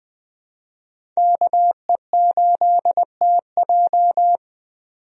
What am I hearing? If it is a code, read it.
Morse code, heard as a keyed tone, about 21 wpm, 700 Hz. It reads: KE8TJ
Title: USA Call Signs - 5 Characters - 20wpm